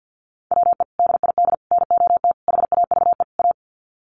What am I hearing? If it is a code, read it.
Morse code, heard as a keyed tone, about 50 wpm, 700 Hz. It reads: WE DID NOT HAVE A
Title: Sets of 5 Words Encore2 50wpm